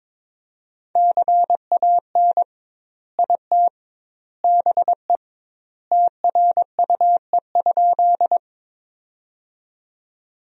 Morse code, heard as a keyed tone, about 22 wpm, 700 Hz. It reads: CAN IT BE TRUE?
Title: 2x Word-Spacing - Sentences from Top 400 Words - 22wpm